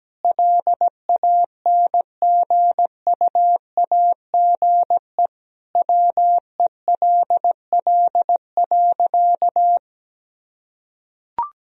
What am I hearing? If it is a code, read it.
Morse code, heard as a keyed tone, about 17 wpm, 700 Hz. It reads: LANGUAGE WELL.
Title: Sentences from Top 1000 Words - Review Part 3 - 17wpm